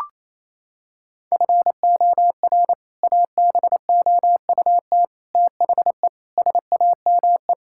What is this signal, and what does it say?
Morse code, heard as a keyed tone, about 27 wpm, 700 Hz. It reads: FOR ABOUT THE SAME
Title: Sentences from Top 1000 Words - Review Part 5 - 28wpm